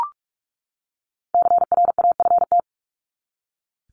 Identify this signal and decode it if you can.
Morse code, heard as a keyed tone, about 44 wpm, 700 Hz. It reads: CRAFT
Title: Top 751-1000 Words 45wpm